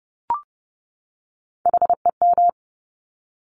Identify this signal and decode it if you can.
Morse code, heard as a keyed tone, about 30 wpm, 700 Hz. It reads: HEM
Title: Top Three Letter Combinations in Words 30wpm